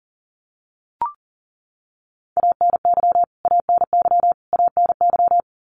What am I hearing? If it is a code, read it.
Morse code, heard as a keyed tone, about 40 wpm, 700 Hz. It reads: ANY ANY ANY
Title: Top 200 Words Repeated 40wpm